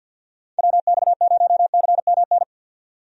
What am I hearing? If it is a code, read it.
Morse code, heard as a keyed tone, about 47 wpm, 700 Hz. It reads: WX0CMN
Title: USA Call Signs - 6 Characters - 50wpm